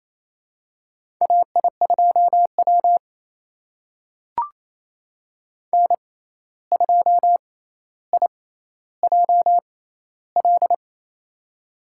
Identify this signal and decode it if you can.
Morse code, heard as a keyed tone, about 28 wpm, spaced as for 13 wpm, 700 Hz. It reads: AI2W N2IJL
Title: USA Call Signs - 6x Character Spacing - 28wpm